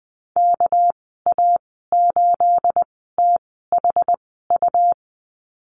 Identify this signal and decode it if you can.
Morse code, heard as a keyed tone, about 20 wpm, spaced as for 15 wpm, 700 Hz. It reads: KA8THU
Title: 231 Lesson 37g - Letter x - Call Signs 20wpm